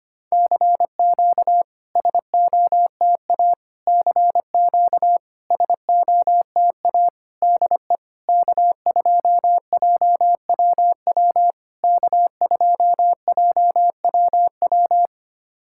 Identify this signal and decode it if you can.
Morse code, heard as a keyed tone, about 25 wpm, 700 Hz. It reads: CQ SOTA CQ SOTA DE K2JWW K2JWW
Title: SOTA QSO Practice 25wpm